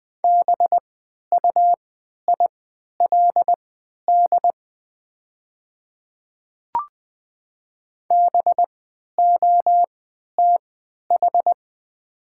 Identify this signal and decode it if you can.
Morse code, heard as a keyed tone, about 20 wpm, spaced as for 11 wpm, 700 Hz. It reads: BUILD BOTH
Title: 3x Character Spacing - Top 300 Words - 20wpm